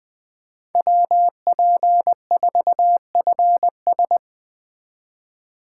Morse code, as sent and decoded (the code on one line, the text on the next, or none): .-- .--. ....- ..-. ...
WP4FS